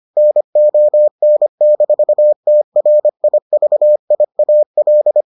Morse code, heard as a keyed tone, about 25 wpm, 600 Hz. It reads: NON-TRIVIAL